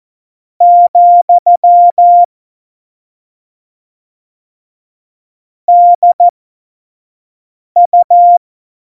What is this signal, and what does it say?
Morse code, heard as a keyed tone, about 14 wpm, spaced as for 5 wpm, 700 Hz. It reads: , DU
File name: Morsekurs / Witze / Witz 04. September 2018, 25 F70